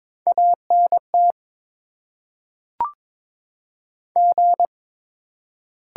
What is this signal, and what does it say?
Morse code, heard as a keyed tone, about 22 wpm, 700 Hz. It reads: A N T G
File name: Top 20 QSO Elements - 8x Character Spacing - 22wpm